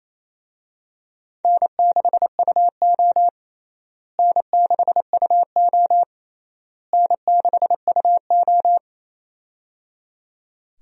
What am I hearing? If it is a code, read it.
Morse code, heard as a keyed tone, about 28 wpm, 700 Hz. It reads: N6UO N6UO N6UO
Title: USA Call Signs - 4 Characters - Repeated 3x - 28wpm